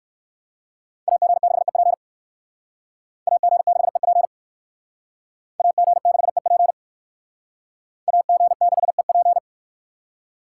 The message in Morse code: AG6EP AG6EP AG6EP AG6EP

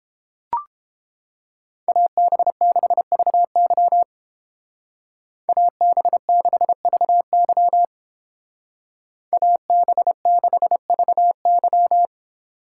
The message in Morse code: AB6VY AB6VY AB6VY

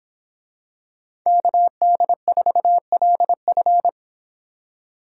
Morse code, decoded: KD4LF

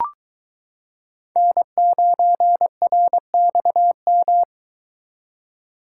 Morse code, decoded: N9RXM

